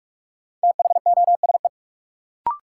Morse code, heard as a keyed tone, about 46 wpm, 700 Hz. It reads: THOSE
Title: Sentences from Top 300 Words - New Word Drill - 45wpm